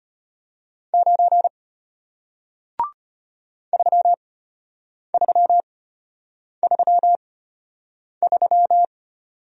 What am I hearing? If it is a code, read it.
Morse code, heard as a keyed tone, about 32 wpm, spaced as for 8 wpm, 700 Hz. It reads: 9 3333